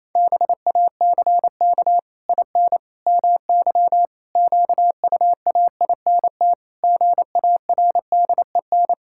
Morse code, heard as a keyed tone, about 27 wpm, 700 Hz. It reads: BACK IN MY QUAINT GARDEN
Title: Pangram Challenge v2 - 28wpm